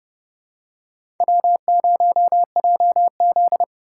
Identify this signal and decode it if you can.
Morse code, heard as a keyed tone, about 32 wpm, 700 Hz. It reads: W0JZ
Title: USA Call Signs - 4 Characters - 6x Character Spacing - 30wpm